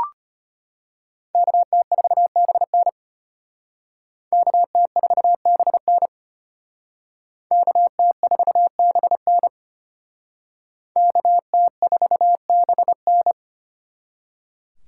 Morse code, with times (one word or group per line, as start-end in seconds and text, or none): 1.3-2.9 KT4BN
4.3-6.1 KT4BN
7.5-9.5 KT4BN
11.0-13.3 KT4BN